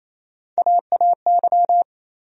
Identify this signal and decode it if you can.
Morse code, heard as a keyed tone, about 28 wpm, 700 Hz. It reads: AAY
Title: USA Call Sign Postfixes - 3 Letters 28wpm